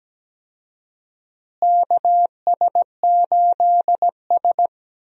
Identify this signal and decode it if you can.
Morse code, heard as a keyed tone, about 17 wpm, 700 Hz. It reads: KS8S
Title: USA Call Signs - 4 Characters - Repeated 3x - 17wpm